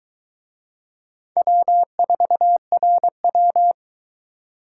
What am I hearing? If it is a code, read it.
Morse code, heard as a keyed tone, about 22 wpm, 700 Hz. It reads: W4RW